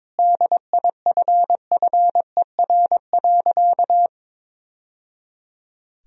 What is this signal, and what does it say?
Morse code, heard as a keyed tone, about 22 wpm, 700 Hz. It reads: DIFFER.
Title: Sentences from Top 1000 Words - Review Part 1 - 22wpm